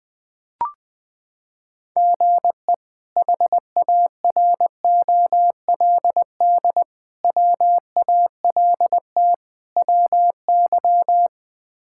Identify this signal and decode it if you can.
Morse code, heard as a keyed tone, about 21 wpm, 700 Hz. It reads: GE HAROLD WALT WY
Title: K1USN SST QSO Practice - 20wpm